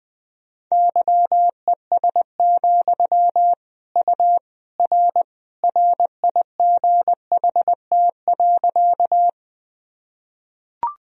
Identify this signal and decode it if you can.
Morse code, heard as a keyed tone, about 20 wpm, 700 Hz. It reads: YES, U R RIGHT.
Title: Sentences from Top 500 Words and Abbreviations 20wpm